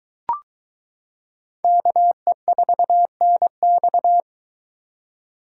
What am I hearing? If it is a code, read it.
Morse code, heard as a keyed tone, about 22 wpm, 700 Hz. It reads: KE4NX